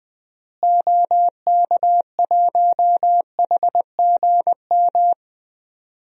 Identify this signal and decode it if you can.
Morse code, heard as a keyed tone, about 20 wpm, 700 Hz. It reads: OK1HGM
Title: DXCC-100 Call Signs - 6 Characters - Repeated 2x - 20wpm